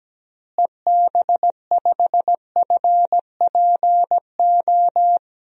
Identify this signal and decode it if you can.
Morse code, heard as a keyed tone, about 17 wpm, 700 Hz. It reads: EB5FPO
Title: DXCC-100 Call Signs - 6 Characters - Repeated 2x - 17wpm